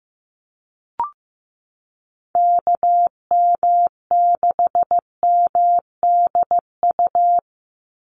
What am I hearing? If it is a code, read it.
Morse code, heard as a keyed tone, about 15 wpm, 700 Hz. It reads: KM6MDU
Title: USA Call Signs 15wpm